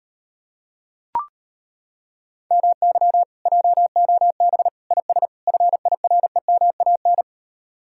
Morse code, heard as a keyed tone, about 33 wpm, 700 Hz. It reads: MY JOB IS FIREMAN